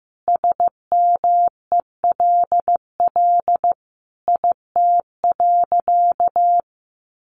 Morse code, heard as a keyed tone, about 14 wpm, 700 Hz. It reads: SMELL IT.
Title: Additional Easy Phrases 15wpm